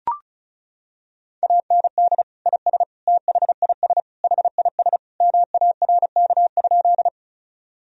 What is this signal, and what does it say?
Morse code, heard as a keyed tone, about 34 wpm, 700 Hz. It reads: AND IS THIS HIS MARK?
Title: Sentences from Top 1000 Words - Review Part 1 - 35wpm